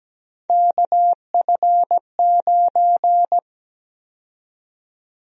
Morse code, heard as a keyed tone, about 17 wpm, 700 Hz. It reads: KF9
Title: USA Call Sign Prefixes - 17wpm